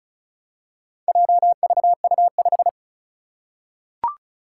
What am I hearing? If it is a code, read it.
Morse code, heard as a keyed tone, about 36 wpm, 700 Hz. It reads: JVU5
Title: Four Challenging Letter-Number Combinations 35wpm